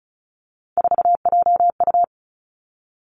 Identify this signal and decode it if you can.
Morse code, heard as a keyed tone, about 36 wpm, 700 Hz. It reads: 4JU
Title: Three Challenging Letter-Number Combinations 35wpm